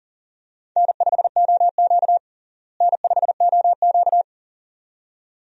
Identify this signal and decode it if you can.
Morse code, heard as a keyed tone, about 37 wpm, 700 Hz. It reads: N5OQ N5OQ
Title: USA Call Signs - 4 Characters - Repeated 2x - 40wpm